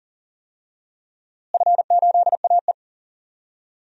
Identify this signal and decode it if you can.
Morse code, heard as a keyed tone, about 40 wpm, 700 Hz. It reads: F8AE
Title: DXCC-100 Call Signs - 4 Characters - Repeated 2x - 40wpm